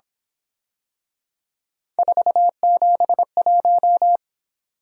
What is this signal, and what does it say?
Morse code, heard as a keyed tone, about 26 wpm, 700 Hz. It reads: 471
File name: Speed-Racing - Three Letter-Number Combinations - 22wpm